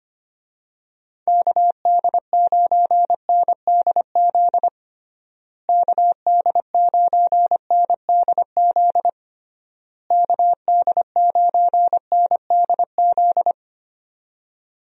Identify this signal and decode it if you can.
Morse code, heard as a keyed tone, about 25 wpm, 700 Hz. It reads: KD9NDZ KD9NDZ KD9NDZ
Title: USA Call Signs - 6 Characters - Repeated 3x - 25wpm